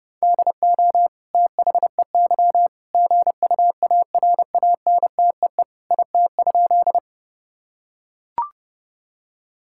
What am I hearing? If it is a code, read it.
Morse code, heard as a keyed tone, about 30 wpm, 700 Hz. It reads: DO THEY GUARANTEE IT?
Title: Sentences from Top 1600 Words 30wpm